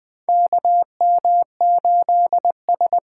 Morse code, heard as a keyed tone, about 20 wpm, 700 Hz. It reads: KM8S
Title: USA Call Signs - 4 Characters - 4x Character Spacing - 20wpm